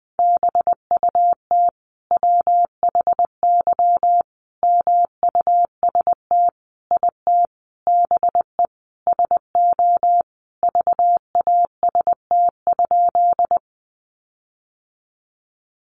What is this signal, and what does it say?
Morse code, heard as a keyed tone, about 19 wpm, 700 Hz. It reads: BUT WHY MUST IT BE SO VAST?
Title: Sentences from Top 1300 Words 20wpm